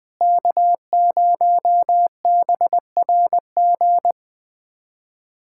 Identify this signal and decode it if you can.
Morse code, heard as a keyed tone, about 20 wpm, 700 Hz. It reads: K0BRG
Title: USA Callsign DE Callsign - 5 Characters - 20wpm